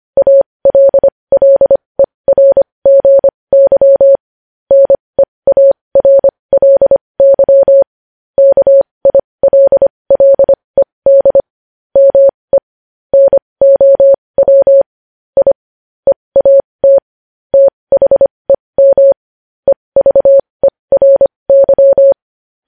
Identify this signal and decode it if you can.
Morse code, heard as a keyed tone, about 25 wpm, spaced as for 20 wpm, 550 Hz. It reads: ALLERGY NEARLY KILLED ME NOW I EAT THEM EVERY
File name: podcast / News headlines-1x-25x20